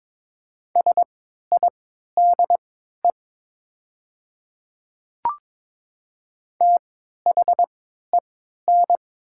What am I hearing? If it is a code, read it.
Morse code, heard as a keyed tone, about 22 wpm, spaced as for 12 wpm, 700 Hz. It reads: SIDE THEN
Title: 3x Character Spacing - Top 100 Words - 22wpm